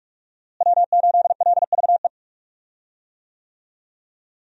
Morse code, heard as a keyed tone, about 46 wpm, 700 Hz. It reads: W8LVE
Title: USA Call Signs - 5 Characters - Repeated 3x - 45wpm